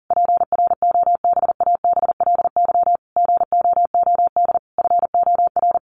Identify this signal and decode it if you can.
Morse code, heard as a keyed tone, about 40 wpm, 700 Hz. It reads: PROBABLY GOOD FOR
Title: Sentences from Top 1400 Words 40wpm